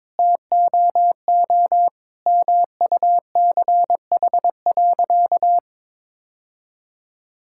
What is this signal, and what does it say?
Morse code, heard as a keyed tone, about 21 wpm, 700 Hz. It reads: TOO MUCH.
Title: Sentences from Top 1000 Words - Review Part 5 - 22wpm